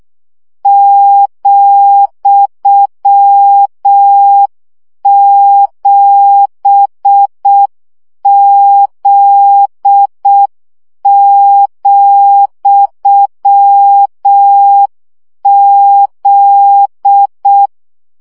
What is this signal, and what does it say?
Morse code, heard as a keyed tone, about 6 wpm, 800 Hz. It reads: ,7Z,Z
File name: CW gyakorlas z 7 vesszo 5 baud